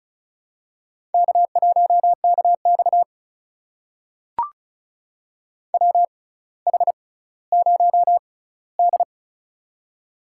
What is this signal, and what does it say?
Morse code, heard as a keyed tone, about 34 wpm, spaced as for 18 wpm, 700 Hz. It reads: K1KX WH0D
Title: USA Call Signs - 4 Characters - 6x Character Spacing - 35wpm